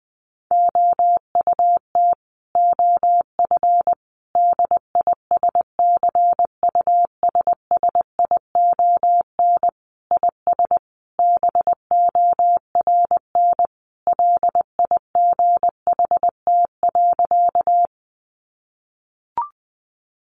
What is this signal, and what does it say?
Morse code, heard as a keyed tone, about 20 wpm, 700 Hz. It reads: OUT OF DISCUSSION IS BORN LIGHT.
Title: Sentences from Top 1100 Words 20wpm